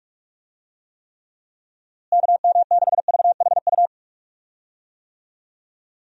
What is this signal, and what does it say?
Morse code, heard as a keyed tone, about 46 wpm, 700 Hz. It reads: KM6VHU
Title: USA Call Signs - 6 Characters - Repeated 3x - 45wpm